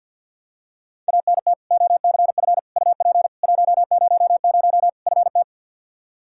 Morse code, heard as a keyed tone, about 46 wpm, 700 Hz. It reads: ANT OCF UP 100 FT
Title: Practice QSO - Exchange 2 - Info - 50wpm